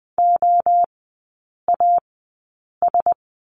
Morse code, heard as a keyed tone, about 20 wpm, 700 Hz. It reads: O A S
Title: 035 Lesson 7e - Letter s - Three Characters 20wpm